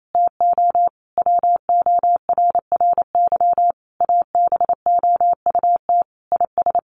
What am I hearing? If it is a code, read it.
Morse code, heard as a keyed tone, about 28 wpm, 700 Hz. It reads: TO WORRY ABOUT IS
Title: A Sound Of Thunder 28wpm